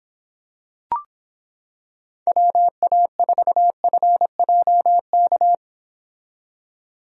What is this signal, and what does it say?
Morse code, heard as a keyed tone, about 26 wpm, 700 Hz. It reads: WA4FJK